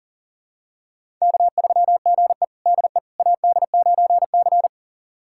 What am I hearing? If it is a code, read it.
Morse code, heard as a keyed tone, about 39 wpm, 700 Hz. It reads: K3GE DE AD9C
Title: USA Callsign DE Callsign - 4 Characters - 40wpm